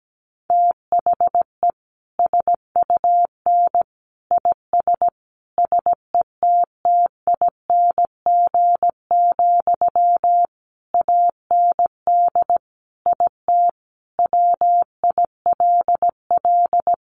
Morse code, heard as a keyed tone, about 17 wpm, 700 Hz. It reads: THE SUN IS SETTING, AND IT WILL
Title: Sentences from Top 1300 Words 17wpm